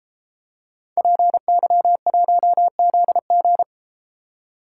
Morse code, heard as a keyed tone, about 33 wpm, 700 Hz. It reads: PY1ZG